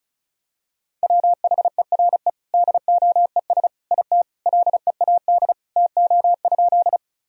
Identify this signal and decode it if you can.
Morse code, heard as a keyed tone, about 36 wpm, 700 Hz. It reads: WHERE DOES IT LEAD TO?
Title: Sentences from Top 1000 Words - Review Part 2 - 35wpm